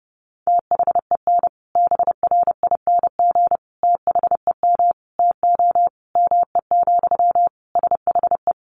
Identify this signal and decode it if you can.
Morse code, heard as a keyed tone, about 30 wpm, 700 Hz. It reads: THEN BRING THEM TO ME, SHE
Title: Sentences from Top 500 Words 30wpm